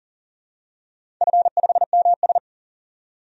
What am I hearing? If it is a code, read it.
Morse code, heard as a keyed tone, about 40 wpm, 700 Hz. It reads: F5MS
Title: DXCC-100 Call Signs - 4 Characters - 40wpm